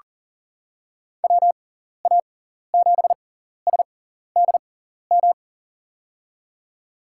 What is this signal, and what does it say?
Morse code, heard as a keyed tone, about 43 wpm, spaced as for 14 wpm, 700 Hz. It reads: WA7SDM
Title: USA Call Signs - 6 Characters - 6x Character Spacing - 40wpm